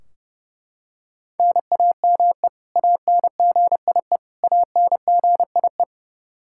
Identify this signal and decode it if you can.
Morse code, heard as a keyed tone, about 30 wpm, 700 Hz. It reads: NAME ANGIE ANGIE
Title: Practice QSO - Exchange 1 - Beginning - 30wpm